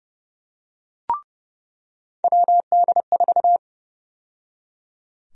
Morse code, heard as a keyed tone, about 30 wpm, 700 Hz. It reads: WD4
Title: USA Call Sign Prefixes - 30wpm